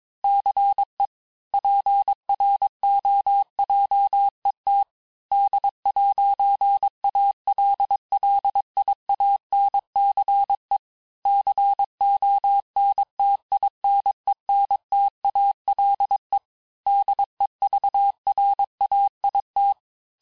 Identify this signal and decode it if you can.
Morse code, heard as a keyed tone, about 22 wpm, 800 Hz. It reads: CE PROJET D'ALLIANCE CONTINENTALE DEVRAIT
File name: cours-du-lundi files / cours-1587-texte-clair-vitesse-16mpm-puis-21mpm